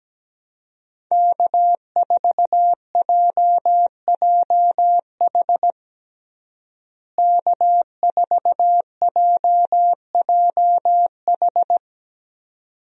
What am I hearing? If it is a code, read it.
Morse code, heard as a keyed tone, about 17 wpm, 700 Hz. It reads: K4JJH K4JJH
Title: USA Call Signs - 5 Characters - Repeated 2x - 17wpm